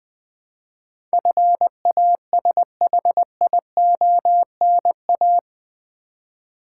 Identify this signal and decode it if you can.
Morse code, heard as a keyed tone, about 20 wpm, 700 Hz. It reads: FASHIONA
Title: Build A Word - 4 Letters At A Time - 11-Letter Words - 20wpm